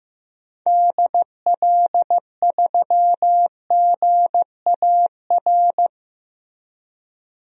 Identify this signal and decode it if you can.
Morse code, heard as a keyed tone, about 15 wpm, 700 Hz. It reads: DL3GAR
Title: DXCC-100 Call Signs - 6 Characters - Repeated 3x - 15wpm